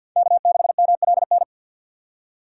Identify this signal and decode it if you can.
Morse code, heard as a keyed tone, about 52 wpm, 690 Hz. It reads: K6MLN
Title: USA Call Signs - 5 Characters - 6x Character Spacing - 50wpm